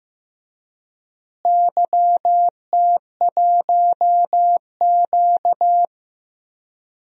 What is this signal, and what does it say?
Morse code, heard as a keyed tone, about 15 wpm, 700 Hz. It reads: YT1Q
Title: DXCC-100 Call Signs - 4 Characters - Repeated 3x - 15wpm